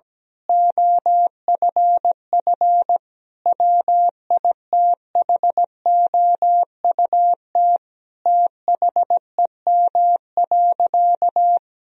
Morse code, heard as a keyed tone, about 17 wpm, 700 Hz. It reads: OFF WITHOUT THEM.
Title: Sentences from Top 1000 Words - Review Part 3 - 17wpm